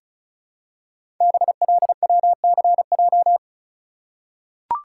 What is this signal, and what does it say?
Morse code, heard as a keyed tone, about 36 wpm, 700 Hz. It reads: BLWCJ
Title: Code Groups - 5 Letters - 35wpm